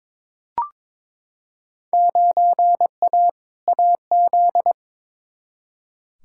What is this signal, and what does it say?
Morse code, heard as a keyed tone, about 22 wpm, 700 Hz. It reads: 9A AZ
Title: Field Day Exchanges - 22wpm